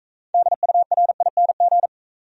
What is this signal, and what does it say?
Morse code, heard as a keyed tone, about 44 wpm, 700 Hz. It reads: DURING